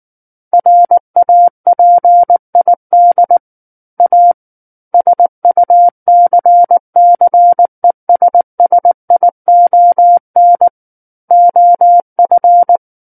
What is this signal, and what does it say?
Morse code, heard as a keyed tone, about 19 wpm, 700 Hz. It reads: RAPID A SUCCESSION OF
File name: Book One / 10-Chapter 10 - 19 WPM